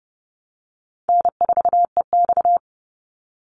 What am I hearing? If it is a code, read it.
Morse code, heard as a keyed tone, about 30 wpm, 700 Hz. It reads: N4EX
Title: NA SOTA Activator Callsigns 30wpm